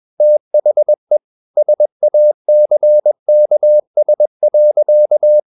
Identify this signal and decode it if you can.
Morse code, heard as a keyed tone, about 21 wpm, 600 Hz. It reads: THE SACKS.
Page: CW med Gnister